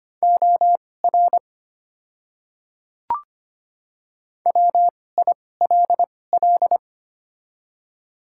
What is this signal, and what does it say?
Morse code, heard as a keyed tone, about 25 wpm, spaced as for 18 wpm, 690 Hz. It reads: OR WILL
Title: 2x Character Spacing - Top 500 Words - 25wpm